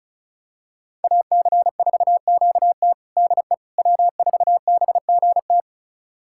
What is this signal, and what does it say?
Morse code, heard as a keyed tone, about 36 wpm, 700 Hz. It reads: AC4QT DE W4BGT
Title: USA Callsign DE Callsign - 5 Characters - 35wpm